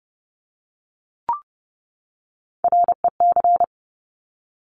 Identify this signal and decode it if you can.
Morse code, heard as a keyed tone, about 30 wpm, 700 Hz. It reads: REC
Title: Top Three Letter Combinations in Words 30wpm